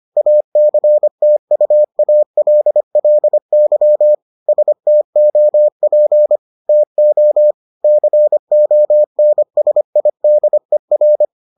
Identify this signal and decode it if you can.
Morse code, heard as a keyed tone, about 25 wpm, 600 Hz. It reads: ACTUALLY STOP TO CONSIDER